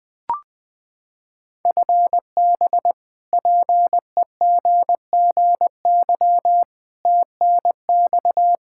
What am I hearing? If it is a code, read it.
Morse code, heard as a keyed tone, about 21 wpm, 700 Hz. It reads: FB PEGGY TNX
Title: Practice QSO - Exchange 3 - More Info - 20wpm